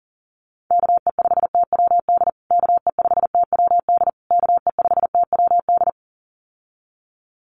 Morse code, heard as a keyed tone, about 37 wpm, 700 Hz. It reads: KE5TWD KE5TWD KE5TWD
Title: USA Call Signs Repeated 3x 40wpm